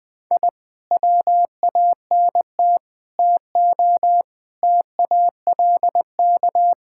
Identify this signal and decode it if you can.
Morse code, heard as a keyed tone, about 20 wpm, 700 Hz. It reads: I WANT TO TALK
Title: Sentences with 4-Word Phrases - 20wpm